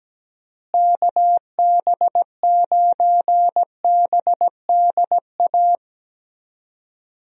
Morse code, heard as a keyed tone, about 17 wpm, 700 Hz. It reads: KB9BDA
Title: USA Call Signs - 6 Characters - Repeated 2x - 17wpm